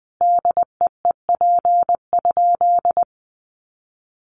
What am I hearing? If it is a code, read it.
Morse code, heard as a keyed tone, about 20 wpm, 700 Hz. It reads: DEEP?
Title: Sentences from Top 1300 Words 20wpm